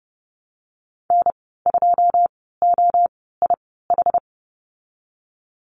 Morse code, heard as a keyed tone, about 32 wpm, spaced as for 17 wpm, 700 Hz. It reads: N2OIH
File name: Lesson 19g - Letter w - Call Signs 30wpm